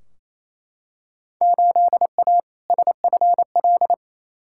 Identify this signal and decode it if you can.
Morse code, heard as a keyed tone, about 28 wpm, 700 Hz. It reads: 8A SFL
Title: Field Day Exchanges Encore - 28wpm